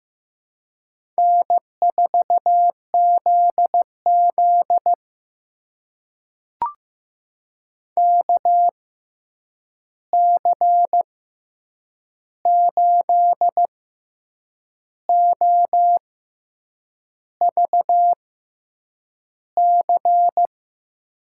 Morse code, heard as a keyed tone, about 15 wpm, spaced as for 7 wpm, 700 Hz. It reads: N4ZZ KC8OVC
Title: USA Call Signs - 6x Character Spacing - 15wpm